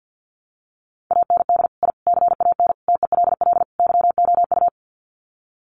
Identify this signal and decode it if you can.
Morse code, heard as a keyed tone, about 49 wpm, 700 Hz. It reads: AND I CAN TELL YOU
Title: Sets of 5 Words Encore2 50wpm